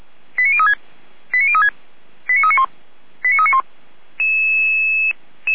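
Five-tone selective call (ZVEI), heard as quick runs of five tones, tones of about 70 ms. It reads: 89036 89036 89391 89391
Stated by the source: ZVEI five-tone selective call received on SDR.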